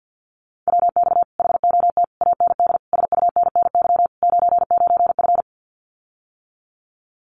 Morse code, heard as a keyed tone, about 50 wpm, 700 Hz. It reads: WX HOT AND SUNNY 99F
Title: Common QSO Phrases Encore 50wpm